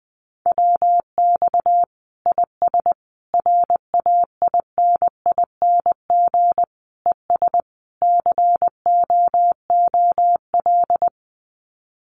Morse code, heard as a keyed tone, about 20 wpm, 700 Hz. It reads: WX IS RAINING ES COOL
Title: Common QSO Phrases 20wpm